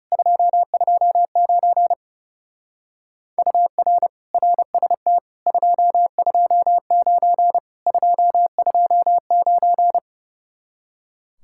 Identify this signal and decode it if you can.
Morse code, heard as a keyed tone, about 36 wpm, 700 Hz. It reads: 229 UR RST 229 229
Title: Speed-Racing Common QSO Phrases - 30wpm